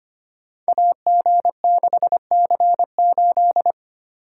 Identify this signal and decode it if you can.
Morse code, heard as a keyed tone, about 25 wpm, 700 Hz. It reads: AG6C8
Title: Code Groups - 5 Letter:Numbers - 25wpm